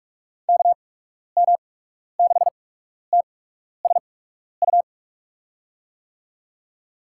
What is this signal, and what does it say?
Morse code, heard as a keyed tone, about 48 wpm, spaced as for 12 wpm, 700 Hz. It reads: KM6TSU
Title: USA Call Signs - 6 Characters - 8x Character Spacing - 45wpm